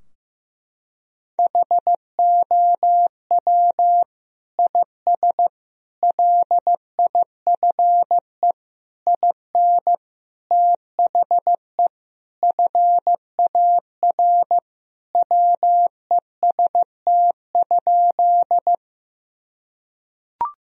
Morse code, heard as a keyed tone, about 15 wpm, 700 Hz. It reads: HOW IS LIFE IN THE FAR WEST?
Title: Sentences from Top 1000 Words - Review Part 2 - 15wpm